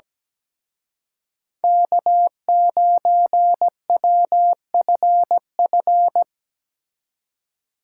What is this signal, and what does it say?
Morse code, heard as a keyed tone, about 20 wpm, 700 Hz. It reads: K9WFF
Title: Speed Racing - USA 1x3 Call Signs 17wpm